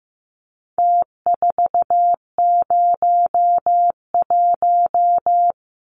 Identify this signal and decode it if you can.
Morse code, heard as a keyed tone, about 15 wpm, 700 Hz. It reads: T401
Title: Numbers - Contest Style - 4-Digit 15wpm